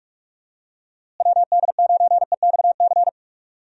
Morse code, heard as a keyed tone, about 46 wpm, 700 Hz. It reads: WD9EXC
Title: USA Call Signs - 6 Characters - Repeated 3x - 45wpm